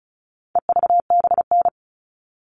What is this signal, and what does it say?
Morse code, heard as a keyed tone, about 34 wpm, 700 Hz. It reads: EVBN
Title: All Cut Numbers - 4-Digit 35wpm